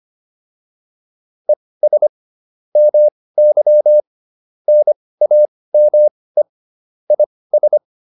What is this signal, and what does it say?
Morse code, heard as a keyed tone, about 25 wpm, spaced as for 18 wpm, 600 Hz. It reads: ES MY NAME IS